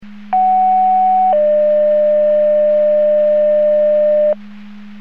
Two-tone page, heard as a one-second tone, then a three-second tone, about 750 then 600 Hz, with noise. It is heard from 0.3 to 4.3 s.
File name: Portage 154.385 Paging